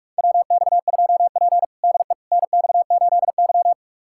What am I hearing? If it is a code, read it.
Morse code, heard as a keyed tone, about 45 wpm, 700 Hz. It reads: WX2P DE NX8Y
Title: USA Callsign DE Callsign - 4 Characters - 45wpm